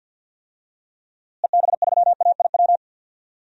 Morse code, heard as a keyed tone, about 51 wpm, 700 Hz. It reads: EB3AIW
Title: DXCC-100 Call Signs - 6 Characters - Repeated 3x - 50wpm